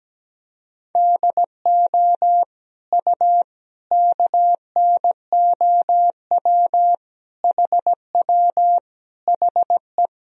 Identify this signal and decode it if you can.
Morse code, heard as a keyed tone, about 17 wpm, 700 Hz. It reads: DO U KNOW HW HE
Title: Sentences from Top 300 Words and Abbreviations 17wpm